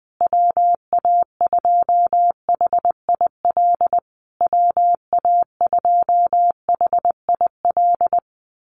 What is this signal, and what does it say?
Morse code, heard as a keyed tone, about 20 wpm, 700 Hz. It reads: WA2HIL WA2HIL
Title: USA Call Signs Repeated 3x 20wpm